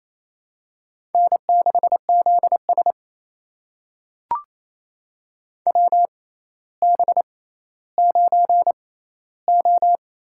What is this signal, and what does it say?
Morse code, heard as a keyed tone, about 28 wpm, spaced as for 14 wpm, 700 Hz. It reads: N6ZS WB9O
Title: USA Call Signs - 4 Characters - 6x Character Spacing - 28wpm